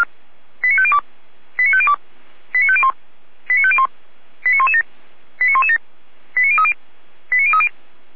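Five-tone selective call (ZVEI), heard as quick runs of five tones, tones of about 80 ms. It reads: 89502 89502 89501 89501 89117 89117 89030 89030
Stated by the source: ZVEI five-tone selective call received by SDR.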